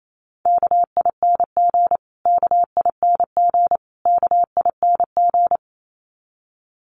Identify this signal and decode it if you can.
Morse code, heard as a keyed tone, about 28 wpm, 700 Hz. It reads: KING KING KING
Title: Top 500 Words Repeated 28wpm